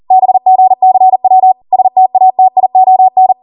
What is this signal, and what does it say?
Morse code, heard as a keyed tone, about 40 wpm, 750 Hz. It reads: BGCW STATION